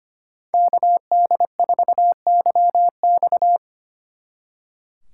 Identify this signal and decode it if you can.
Morse code, heard as a keyed tone, about 29 wpm, 700 Hz. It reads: KD4YX
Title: Speed Racing - USA 2x2 Call Signs 25wpm